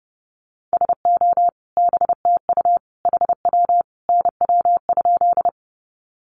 Morse code, heard as a keyed tone, about 28 wpm, 700 Hz. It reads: SO BTU HW NW?
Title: Common QSO Phrases 30wpm